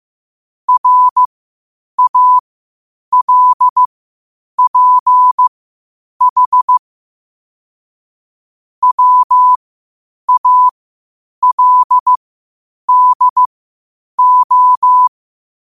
Morse code, heard as a keyed tone, about 15 wpm, spaced as for 8 wpm, 1000 Hz. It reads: RALPH WALDO
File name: qotdfiles / 8wpm